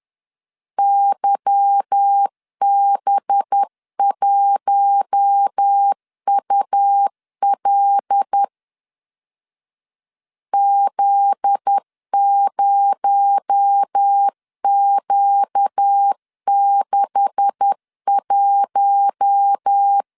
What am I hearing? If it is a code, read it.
Morse code, heard as a keyed tone, about 10 wpm, 800 Hz. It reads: YB1UL Z0Q61
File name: CW / DET / 1 9 1500C S B 10wpm 4439